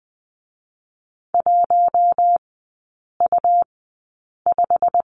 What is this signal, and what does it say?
Morse code, heard as a keyed tone, about 20 wpm, spaced as for 8 wpm, 690 Hz. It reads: 1U5
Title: 133 Lesson 22e - Question Mark - Three Characters 20wpm